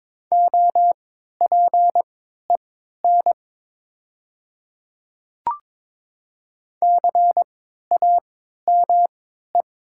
Morse code, heard as a keyed tone, about 22 wpm, spaced as for 12 wpm, 700 Hz. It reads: OPEN CAME
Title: 3x Character Spacing - Top 300 Words - 22wpm